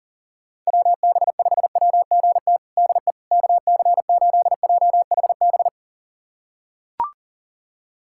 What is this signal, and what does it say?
Morse code, heard as a keyed tone, about 40 wpm, 700 Hz. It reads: WB5WGT DE KC8JHB
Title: USA Callsign DE Callsign 40wpm